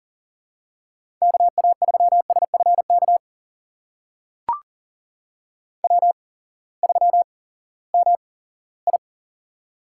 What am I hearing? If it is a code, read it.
Morse code, heard as a keyed tone, about 40 wpm, spaced as for 20 wpm, 700 Hz. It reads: KA3SFK W3MI
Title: USA Call Signs - 8x Character Spacing - 40wpm